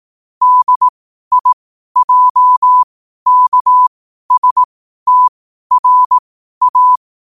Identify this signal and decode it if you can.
Morse code, heard as a keyed tone, about 18 wpm, spaced as for 12 wpm, 1000 Hz. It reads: DIJKSTRA
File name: qotdfiles / 12wpm